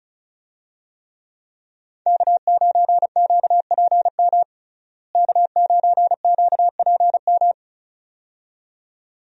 Morse code, heard as a keyed tone, about 34 wpm, 700 Hz. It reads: K9QPM K9QPM
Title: USA Call Signs - 5 Characters - Repeated 2x - 35wpm